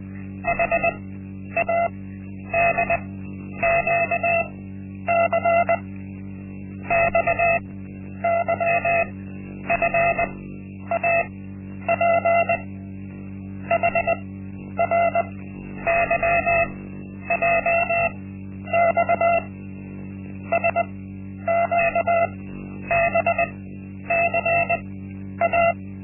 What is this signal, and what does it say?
Morse code, heard as a keyed tone, about 20 wpm, spaced as for 10 wpm, 680 Hz. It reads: HADQC XYFAP HRYJX SQBCA
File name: Linksmakalnis-1